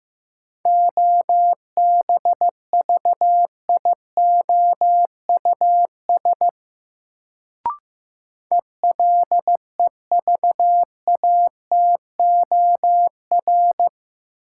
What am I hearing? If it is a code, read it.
Morse code, heard as a keyed tone, about 15 wpm, 700 Hz. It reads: OBVIOUS ELEVATOR
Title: Sentences from Top 1600 Words - New Word Drill - 15wpm